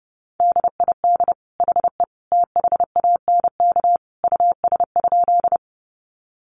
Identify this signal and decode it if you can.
Morse code, heard as a keyed tone, about 30 wpm, 700 Hz. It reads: DID HE THANK US?
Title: Sentences from Top 600 Words 30wpm